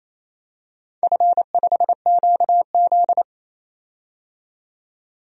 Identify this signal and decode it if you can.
Morse code, heard as a keyed tone, about 29 wpm, 700 Hz. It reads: F5QZ
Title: DXCC-100 Call Signs - 4 Characters - 28wpm